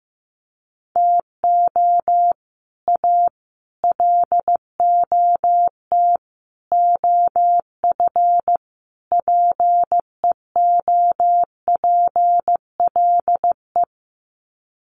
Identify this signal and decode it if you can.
Morse code, heard as a keyed tone, about 15 wpm, 700 Hz. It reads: TO A LOT OF PEOPLE
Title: Sets of 5 Words Encore 15wpm